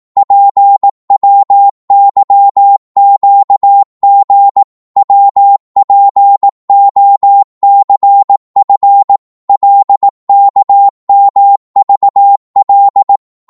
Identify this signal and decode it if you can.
Morse code, heard as a keyed tone, about 18 wpm, 800 Hz. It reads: PWYQG WPOCF LKMVL
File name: Morse / ltr-18wpm